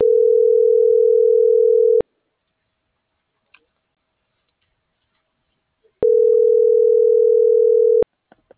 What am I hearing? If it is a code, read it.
Ringback tone (North America), 440 + 480 Hz.